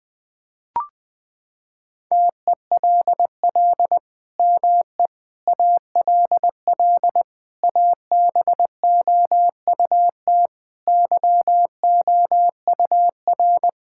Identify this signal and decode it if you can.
Morse code, heard as a keyed tone, about 20 wpm, 700 Hz. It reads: TELL ME ALL ABOUT YOUR
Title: Sentences from Top 1000 Words - Review Part 5 - 20wpm